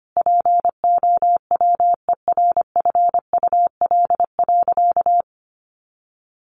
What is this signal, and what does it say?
Morse code, heard as a keyed tone, about 25 wpm, 700 Hz. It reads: POWERFUL.
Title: Sentences from Top 1300 Words 25wpm